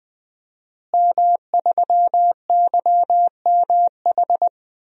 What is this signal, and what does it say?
Morse code, heard as a keyed tone, about 20 wpm, 700 Hz. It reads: M3YMH
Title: DXCC-100 Call Signs - 5 Characters - Repeated 2x - 20wpm